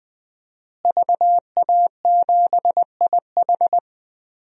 Morse code, heard as a keyed tone, about 20 wpm, 700 Hz. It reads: VA7IH
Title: DXCC-100 Call Signs - 5 Characters - Repeated 2x - 20wpm